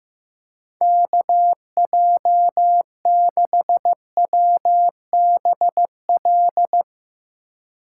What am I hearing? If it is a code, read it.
Morse code, heard as a keyed tone, about 16 wpm, 700 Hz. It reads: KJ6WBL
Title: Speed Racing - USA Call Signs 15wpm